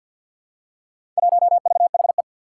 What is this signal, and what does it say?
Morse code, heard as a keyed tone, about 51 wpm, 700 Hz. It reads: 1VHE
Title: Four Challenging Letter-Number Combinations 50wpm